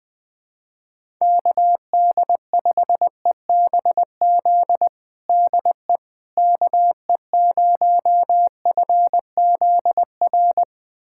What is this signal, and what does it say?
Morse code, heard as a keyed tone, about 20 wpm, 700 Hz. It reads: KD5EBZ DE KE0FZR
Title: USA Callsign DE Callsign - 6 Characters - 20wpm